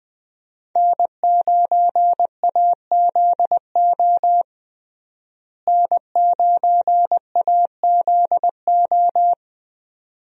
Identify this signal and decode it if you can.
Morse code, heard as a keyed tone, about 20 wpm, 700 Hz. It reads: N9AZO N9AZO
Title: USA Call Signs - 5 Characters - Repeated 2x - 20wpm